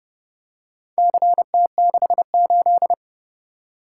Morse code, heard as a keyed tone, about 29 wpm, 700 Hz. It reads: CT68